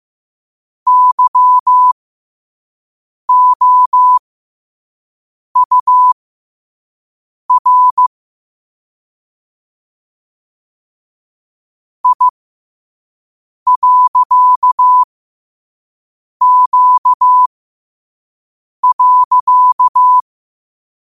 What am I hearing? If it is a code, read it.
Morse code, heard as a keyed tone, about 15 wpm, spaced as for 5 wpm, 1000 Hz. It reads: YOUR I.Q.
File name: qotdfiles / 5wpm